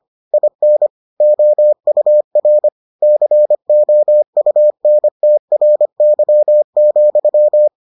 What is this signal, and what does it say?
Morse code, heard as a keyed tone, about 25 wpm, 600 Hz. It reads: IN OUR COUNTRY,